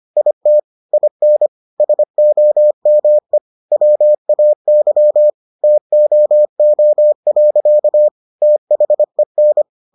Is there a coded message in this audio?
Yes: Morse code from 0.2 to 9.6 s, heard as a keyed tone, about 25 wpm, 600 Hz. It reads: IT IN SOME WAY TOO. THEN